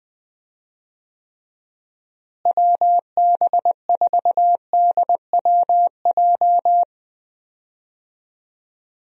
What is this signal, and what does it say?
Morse code, heard as a keyed tone, about 20 wpm, 700 Hz. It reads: WB4DWJ
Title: USA Call Signs - 6 Characters - Repeated 2x - 20wpm